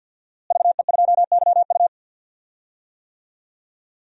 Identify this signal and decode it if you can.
Morse code, heard as a keyed tone, about 46 wpm, 700 Hz. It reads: VE2YU
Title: DXCC-100 Call Signs - 5 Characters - Repeated 2x - 50wpm